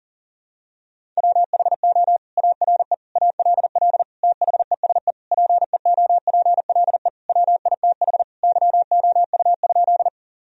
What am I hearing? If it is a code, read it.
Morse code, heard as a keyed tone, about 40 wpm, 700 Hz. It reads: WHO ARE ALL THESE PEOPLE WITH YOU?
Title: Sentences from Top 1000 Words - Review Part 1 - 40wpm